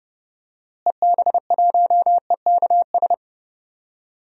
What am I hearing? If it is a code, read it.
Morse code, heard as a keyed tone, about 30 wpm, 700 Hz. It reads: EB1EKS
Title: DXCC-100 Call Signs - 6 Characters - Repeated 2x - 30wpm